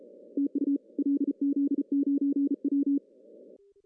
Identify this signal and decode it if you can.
Morse code, heard as a keyed tone, about 35 wpm, 280 Hz. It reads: TU LZ9W
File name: CW / cqww-cw-k5tr